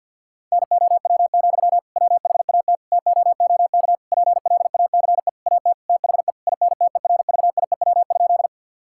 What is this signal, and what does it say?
Morse code, heard as a keyed tone, about 50 wpm, 700 Hz. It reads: NOW, WHAT TOOK PLACE AT THE INTERVIEW?
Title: Sentences from Top 1600 Words 50wpm